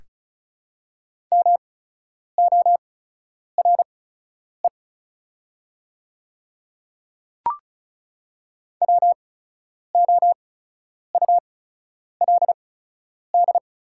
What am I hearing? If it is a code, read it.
Morse code, heard as a keyed tone, about 34 wpm, spaced as for 10 wpm, 690 Hz. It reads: MORE WOULD
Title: 8x Character Spacing - Top 100 Words - 35wpm